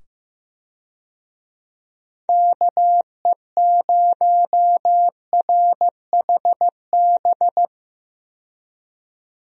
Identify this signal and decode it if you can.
Morse code, heard as a keyed tone, about 15 wpm, 700 Hz. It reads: KE0RHB
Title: USA Call Signs - 6 Characters - 15wpm